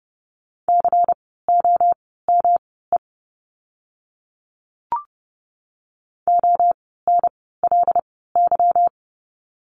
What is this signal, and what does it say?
Morse code, heard as a keyed tone, about 30 wpm, spaced as for 17 wpm, 700 Hz. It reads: COME ONLY
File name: course / 30:10wpm / 237 Lesson 38f - Letter q - Words 30wpm